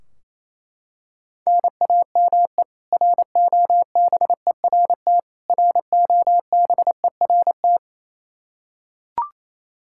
Morse code, heard as a keyed tone, about 28 wpm, 700 Hz. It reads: NAME ROBERT ROBERT
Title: Practice QSO - Exchange 1 - Beginning - 28wpm